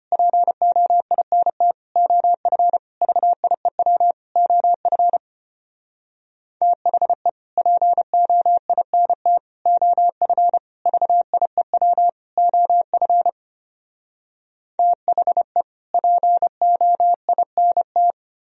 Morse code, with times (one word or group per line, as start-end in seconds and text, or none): 0.1-1.7 POINT
2.0-2.8 OF
3.0-4.1 VIEW
4.4-5.2 OF
6.6-7.3 THE
7.6-9.4 POINT
9.7-10.6 OF
10.9-12.1 VIEW
12.4-13.3 OF
14.8-15.6 THE
15.9-18.1 POINT